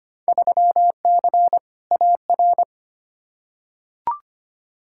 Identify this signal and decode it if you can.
Morse code, heard as a keyed tone, about 25 wpm, 700 Hz. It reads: 3C AR
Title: Field Day Exchanges Encore - 25wpm